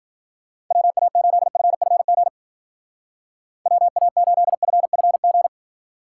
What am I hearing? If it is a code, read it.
Morse code, heard as a keyed tone, about 46 wpm, 700 Hz. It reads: WA8FFG WA8FFG
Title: Speed Racing - USA Call Signs 40wpm